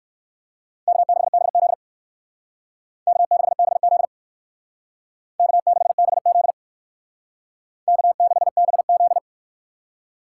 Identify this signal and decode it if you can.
Morse code, heard as a keyed tone, about 56 wpm, 700 Hz. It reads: K6BZ K6BZ K6BZ K6BZ
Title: Speed Racing - USA 1x2 Call Signs 45wpm